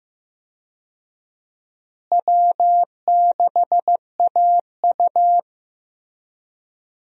Morse code, heard as a keyed tone, about 16 wpm, 700 Hz. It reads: W6AU
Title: USA Call Signs - 4 Characters - Repeated 2x - 15wpm